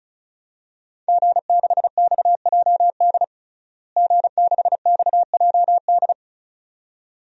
Morse code, heard as a keyed tone, about 36 wpm, 700 Hz. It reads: G6XJD G6XJD
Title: DXCC-100 Call Signs - 5 Characters - Repeated 2x - 35wpm